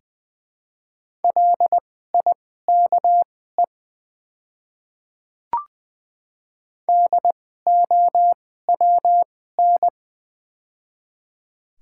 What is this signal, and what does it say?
Morse code, heard as a keyed tone, about 20 wpm, spaced as for 15 wpm, 700 Hz. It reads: LIKE DOWN
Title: 2x Character Spacing - Top 100 Words - 20wpm